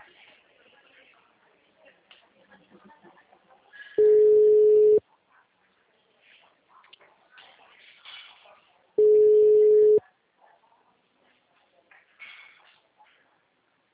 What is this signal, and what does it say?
Ringback tone (Europe, 425 Hz), with speech.